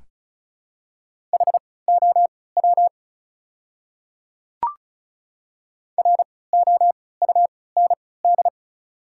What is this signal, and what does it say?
Morse code, heard as a keyed tone, about 34 wpm, spaced as for 20 wpm, 690 Hz. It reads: HOW ROUND
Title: 3x Character Spacing - Top 200 Words - 35wpm